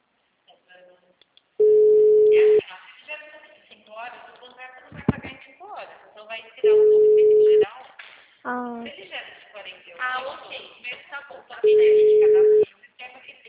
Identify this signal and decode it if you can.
Ringback tone (Europe, 425 Hz), with speech.